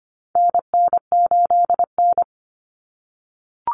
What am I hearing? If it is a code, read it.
Morse code, heard as a keyed tone, about 25 wpm, 700 Hz. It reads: NN8N
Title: Common Cut Numbers - 4-Digit 25wpm